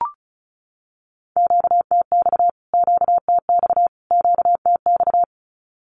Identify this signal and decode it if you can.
Morse code, heard as a keyed tone, about 34 wpm, 700 Hz. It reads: QTX QTX QTX
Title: Q-Codes Repeated 3x 35wpm